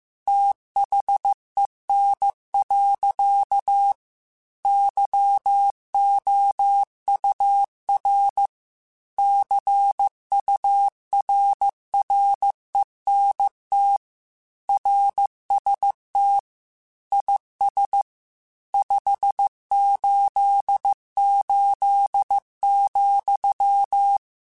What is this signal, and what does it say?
Morse code, heard as a keyed tone, about 15 wpm, 800 Hz. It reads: THEN. YOUR CURRENT RST IS 588,
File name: QSOs / QSOs-mp3s / WA7QOX 1@13